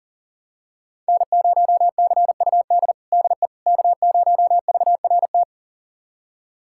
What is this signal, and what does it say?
Morse code, heard as a keyed tone, about 40 wpm, 700 Hz. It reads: N0CUD DE K0VRT
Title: USA Callsign DE Callsign - 5 Characters - 40wpm